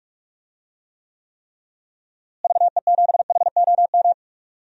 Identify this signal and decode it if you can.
Morse code, heard as a keyed tone, about 44 wpm, 700 Hz. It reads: VE7HOM
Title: DXCC-100 Call Signs - 6 Characters - Repeated 3x - 45wpm